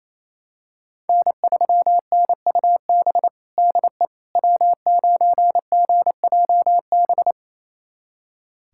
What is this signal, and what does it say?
Morse code, heard as a keyed tone, about 28 wpm, 700 Hz. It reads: N3NUB DE W9GJB
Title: USA Callsign DE Callsign - 5 Characters - 28wpm